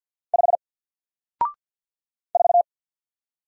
Morse code, heard as a keyed tone, about 48 wpm, 700 Hz. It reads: 5 4
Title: Single Character (Letter:Number:Punctuation:Prosign) Alt 50wpm